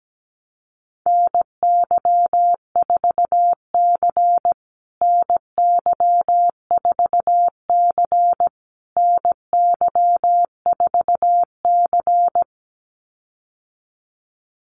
Morse code, heard as a keyed tone, about 17 wpm, 700 Hz. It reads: NY4C NY4C NY4C
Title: USA Call Signs Repeated 3x 17wpm